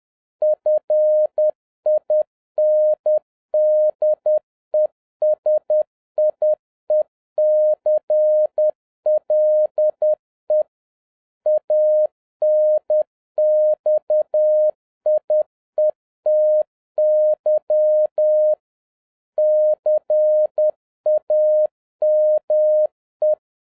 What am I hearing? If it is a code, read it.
Morse code, heard as a keyed tone, about 10 wpm, 610 Hz. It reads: FIN DE SIECLE ANXIETY CAME
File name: 2011-07-mp3 / n31 THE 3.11 2 10wpm